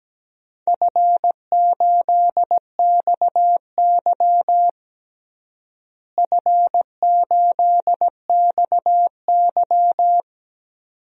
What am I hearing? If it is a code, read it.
Morse code, heard as a keyed tone, about 17 wpm, 700 Hz. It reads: F8XY F8XY
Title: DXCC-100 Call Signs - 4 Characters - Repeated 2x - 17wpm